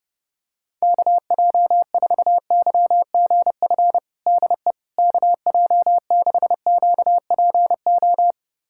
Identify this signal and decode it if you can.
Morse code, heard as a keyed tone, about 30 wpm, 690 Hz. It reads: KJ4YGF DE KJ6QPO
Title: USA Callsign DE Callsign 30wpm